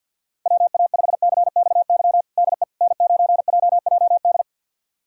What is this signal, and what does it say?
Morse code, heard as a keyed tone, about 52 wpm, 690 Hz. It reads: WA5CXY DE N9JJD
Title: USA Callsign DE Callsign 50wpm